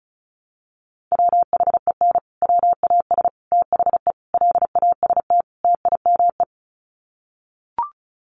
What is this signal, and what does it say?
Morse code, heard as a keyed tone, about 34 wpm, 700 Hz. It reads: WHEN WAS THE LAST TIME
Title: Sets of 5 Words Encore 35wpm